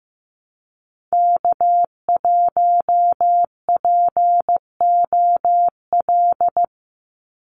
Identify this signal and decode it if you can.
Morse code, heard as a keyed tone, about 15 wpm, 700 Hz. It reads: K1POL
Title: USA Call Signs 15wpm